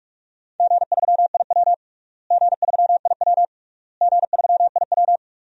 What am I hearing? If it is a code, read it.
Morse code, heard as a keyed tone, about 46 wpm, 700 Hz. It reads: G3IW G3IW G3IW
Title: DXCC-100 Call Signs - 4 Characters - Repeated 3x - 45wpm